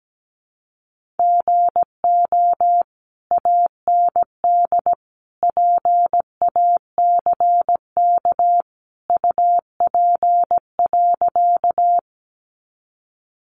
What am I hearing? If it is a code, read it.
Morse code, heard as a keyed tone, about 17 wpm, 700 Hz. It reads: GO AND PACK UP.
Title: Sentences from Top 1300 Words 17wpm